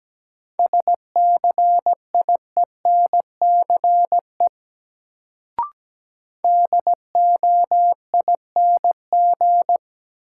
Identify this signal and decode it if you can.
Morse code, heard as a keyed tone, about 17 wpm, 700 Hz. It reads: SCIENCE DOING
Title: Sentences from Top 1400 Words - New Word Drill Alt - 17wpm